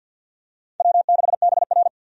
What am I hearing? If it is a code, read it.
Morse code, heard as a keyed tone, about 52 wpm, 700 Hz. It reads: W6BR
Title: USA Call Signs - 4 Characters - 50wpm